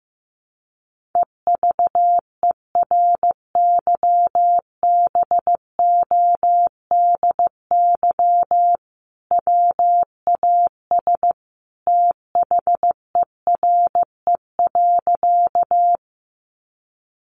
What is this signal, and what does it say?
Morse code, heard as a keyed tone, about 15 wpm, 700 Hz. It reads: EVERYBODY WAS THERE.
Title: Sentences from Top 1400 Words 15wpm